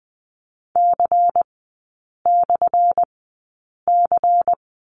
Morse code, heard as a keyed tone, about 20 wpm, spaced as for 8 wpm, 700 Hz. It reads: C/C
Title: 212 Lesson 34e - Number 8 - Three Characters 20wpm